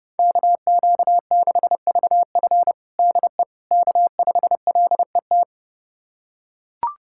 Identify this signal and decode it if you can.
Morse code, heard as a keyed tone, about 30 wpm, 700 Hz. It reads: KQ6VF DE K5LET
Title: USA Callsign DE Callsign - 5 Characters - 30wpm